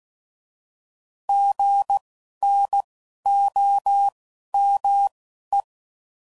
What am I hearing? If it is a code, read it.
Morse code, heard as a keyed tone, about 16 wpm, spaced as for 11 wpm, 800 Hz. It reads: GNOME
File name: Prefixes / By First Letter